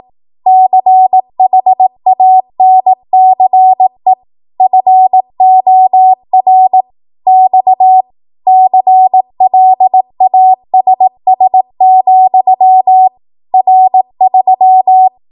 Morse code, heard as a keyed tone, about 18 wpm, 750 Hz. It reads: CHANCE FOR X CLASS, R3
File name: Morse / 18WPM